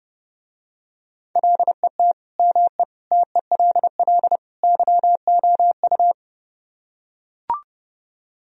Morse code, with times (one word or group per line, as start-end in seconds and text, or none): 1.4-2.1 LET
2.4-2.8 ME
3.1-4.4 TELL
4.6-6.1 YOU